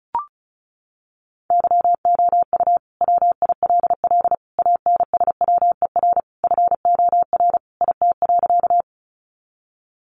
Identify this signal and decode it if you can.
Morse code, heard as a keyed tone, about 34 wpm, 700 Hz. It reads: YOU WILL ANSWER FOR IT.
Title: Sentences from Top 300 Words 35wpm